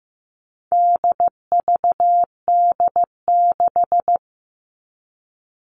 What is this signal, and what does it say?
Morse code, heard as a keyed tone, about 15 wpm, 700 Hz. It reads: DVD6
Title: All Cut Numbers - 4-Digit 15wpm